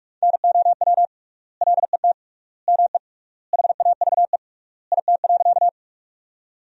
Morse code, heard as a keyed tone, about 46 wpm, 700 Hz. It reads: NOW LET ME HAVE IT.
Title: 3x Word-Spacing - Sentences from Top 300 Words - 45wpm